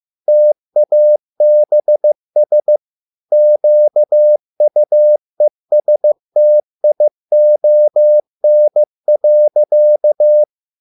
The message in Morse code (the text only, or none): TABS QUESTION.